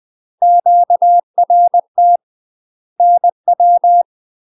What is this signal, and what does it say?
Morse code, heard as a keyed tone, about 20 wpm, 700 Hz. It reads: QRT NW
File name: UK MASTER QSOs TRAINING / 20WPM / UK MASTER QSOs 0007